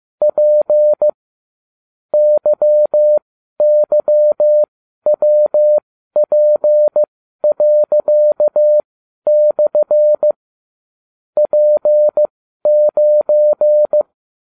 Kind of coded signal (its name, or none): Morse code